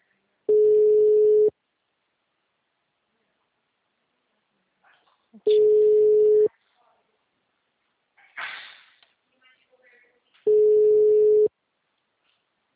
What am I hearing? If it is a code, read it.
Ringback tone (Europe, 425 Hz).